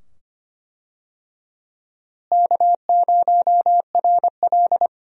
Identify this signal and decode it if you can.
Morse code, heard as a keyed tone, about 25 wpm, 700 Hz. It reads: K0RL
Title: USA Call Signs - 4 Characters - Repeated 2x - 25wpm